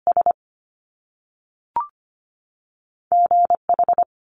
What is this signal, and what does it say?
Morse code, heard as a keyed tone, about 25 wpm, 700 Hz. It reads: S GH
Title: Top Two Letter Combinations in Words 25wpm